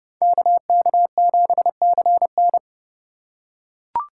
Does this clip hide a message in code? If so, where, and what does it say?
Yes: Morse code from 0.2 to 2.6 s, heard as a keyed tone, about 30 wpm, 700 Hz. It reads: KK7CN